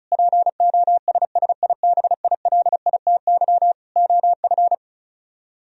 Morse code, heard as a keyed tone, about 43 wpm, 700 Hz. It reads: POSSIBILITY OF
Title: Speed-Racing - Sets of 3 Words 35wpm